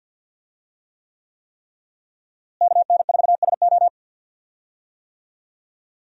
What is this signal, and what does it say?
Morse code, heard as a keyed tone, about 52 wpm, 700 Hz. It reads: KN4SO
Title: USA Call Signs - 5 Characters - Repeated 3x - 50wpm